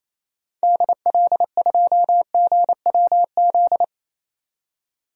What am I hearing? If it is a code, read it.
Morse code, heard as a keyed tone, about 27 wpm, 700 Hz. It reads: DL2GWZ
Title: DXCC-100 Call Signs - 6 Characters - 28wpm